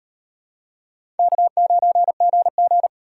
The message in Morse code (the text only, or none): K9GG